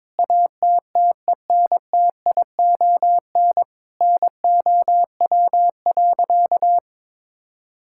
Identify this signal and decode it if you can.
Morse code, heard as a keyed tone, about 22 wpm, 700 Hz. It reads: ATTENTION NOW.
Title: Sentences from Top 1000 Words - Review Part 6 - 22wpm